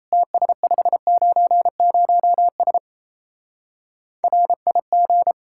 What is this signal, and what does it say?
Morse code, heard as a keyed tone, about 34 wpm, 700 Hz. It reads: TS590S RIG
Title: Speed-Racing Common QSO Phrases Encore - 28wpm